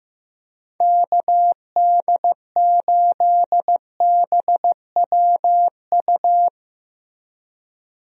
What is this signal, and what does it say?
Morse code, heard as a keyed tone, about 15 wpm, 700 Hz. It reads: KD8BWU
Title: USA Call Signs - 6 Characters - Repeated 2x - 15wpm